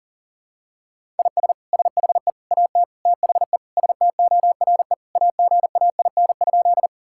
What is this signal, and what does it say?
Morse code, heard as a keyed tone, about 37 wpm, 700 Hz. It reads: IS SHE AT THE STORE AGAIN?
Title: Sentences from Top 1000 Words - Review Part 5 - 40wpm